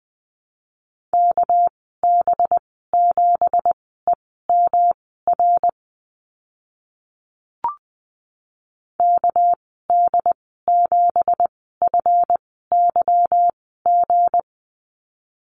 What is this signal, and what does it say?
Morse code, heard as a keyed tone, about 20 wpm, spaced as for 15 wpm, 700 Hz. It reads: KB7EMR KD7FYG
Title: 245 Lesson 39g - Letter z - Call Signs 20wpm